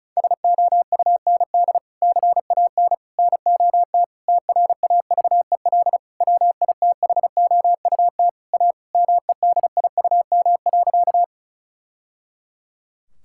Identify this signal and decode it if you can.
Morse code, heard as a keyed tone, about 36 wpm, 700 Hz. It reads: SOUND CAN NOT TRAVEL WITHOUT A MEDIUM.
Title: Sentences from Top 1000 Words - Review Part 6 - 35wpm